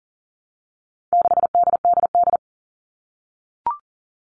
Morse code, heard as a keyed tone, about 40 wpm, 700 Hz. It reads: 6DDD
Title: All Cut Numbers - 4-Digit 40wpm